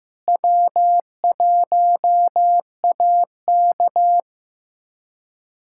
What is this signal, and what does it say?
Morse code, heard as a keyed tone, about 15 wpm, 700 Hz. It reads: W1AK
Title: USA Callsign DE Callsign - 4 Characters - 15wpm